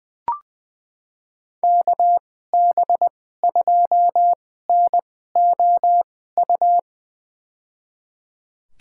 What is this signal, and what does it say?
Morse code, heard as a keyed tone, about 20 wpm, spaced as for 15 wpm, 700 Hz. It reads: KB2NOU
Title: USA Call Signs - 6 Characters - 2x Character Spacing - 20wpm